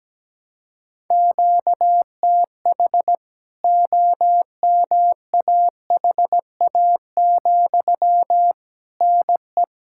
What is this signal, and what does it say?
Morse code, heard as a keyed tone, about 17 wpm, 700 Hz. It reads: QTH OMAHA, NE
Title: K1USN SST QSO Practice 17wpm